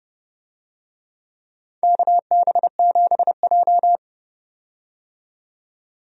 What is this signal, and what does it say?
Morse code, heard as a keyed tone, about 31 wpm, 700 Hz. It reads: KB7J
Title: USA Call Signs - 4 Characters - 30wpm